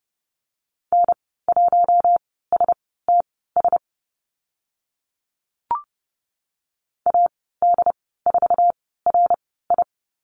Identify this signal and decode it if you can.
Morse code, heard as a keyed tone, about 30 wpm, spaced as for 17 wpm, 700 Hz. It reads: N1STS AD4RI
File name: Lesson 14g - Number 2 - Call Signs 30wpm